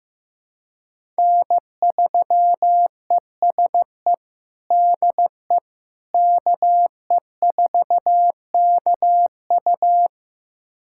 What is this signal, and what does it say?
Morse code, heard as a keyed tone, about 15 wpm, 700 Hz. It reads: N3ESE DE KE4KU
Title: USA Callsign DE Callsign - 5 Characters - 15wpm